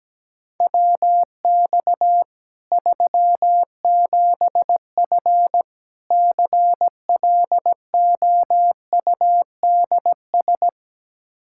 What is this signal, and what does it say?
Morse code, heard as a keyed tone, about 17 wpm, 700 Hz. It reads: WX 37F CLOUDS
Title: Practice QSO - Exchange 2 - Info - 17wpm